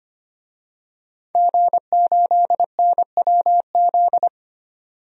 Morse code, heard as a keyed tone, about 25 wpm, 700 Hz. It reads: G8NWZ
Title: DXCC-100 Call Signs - 5 Characters - 25wpm